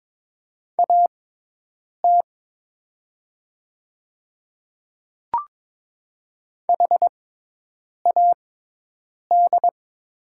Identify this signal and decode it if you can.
Morse code, heard as a keyed tone, about 21 wpm, spaced as for 8 wpm, 700 Hz. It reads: AT HAD
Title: 6x Character Spacing - Top 300 Words - 22wpm